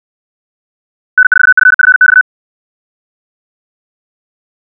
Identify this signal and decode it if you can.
Morse code, heard as a keyed tone, about 64 wpm, 1500 Hz. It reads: N4FFX
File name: ja7uhv / SampleHST3